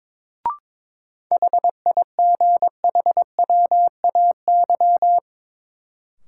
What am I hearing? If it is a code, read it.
Morse code, heard as a keyed tone, about 21 wpm, 700 Hz. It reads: HIGHWAY
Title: Sentences from Top 1400 Words - New Word Drill - 22wpm